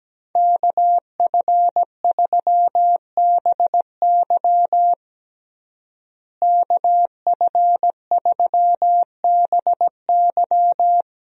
Morse code, heard as a keyed tone, about 17 wpm, 700 Hz. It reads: KF3BY KF3BY
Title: USA Call Signs - 5 Characters - Repeated 3x - 17wpm